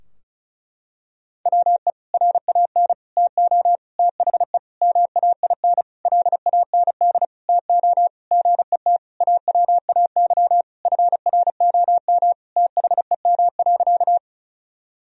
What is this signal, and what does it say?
Morse code, heard as a keyed tone, about 36 wpm, 700 Hz. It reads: WE RAN TO THE MAIN LAND TO GET AWAY FROM THEM.
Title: Sentences from Top 1000 Words - Review Part 2 - 35wpm